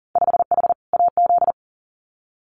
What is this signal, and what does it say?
Morse code, heard as a keyed tone, about 38 wpm, 700 Hz. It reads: 5H AZ
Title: Winter Field Day Exchanges Encore 40wpm